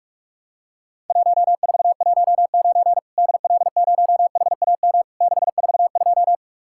Morse code, heard as a keyed tone, about 46 wpm, 700 Hz. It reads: 1419 DL0HAM 642
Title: Worked All Europe DX Contest - QTC Exchange - 45wpm